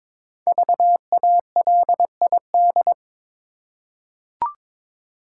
Morse code, heard as a keyed tone, about 22 wpm, 690 Hz. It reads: VALID
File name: 5 Letter Words - First 3 Letters Given - 22wpm